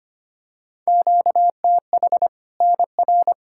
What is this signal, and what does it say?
Morse code, heard as a keyed tone, about 25 wpm, 700 Hz. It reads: QTH NR
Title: Practice QSO - Exchange 1 - Beginning - 25wpm